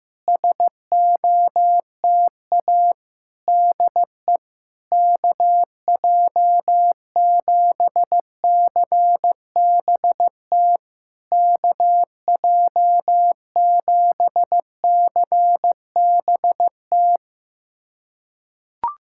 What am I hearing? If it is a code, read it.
Morse code, heard as a keyed tone, about 15 wpm, 700 Hz. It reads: SOTA DE KJ7CBT KJ7CBT
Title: SOTA QSO Practice 15wpm